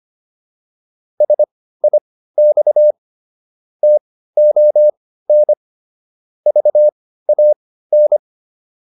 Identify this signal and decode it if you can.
Morse code, heard as a keyed tone, about 25 wpm, spaced as for 15 wpm, 600 Hz. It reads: SIX TON VAN